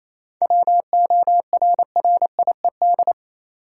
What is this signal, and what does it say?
Morse code, heard as a keyed tone, about 27 wpm, 700 Hz. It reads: WORRIED
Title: Seven Letter Words - 28wpm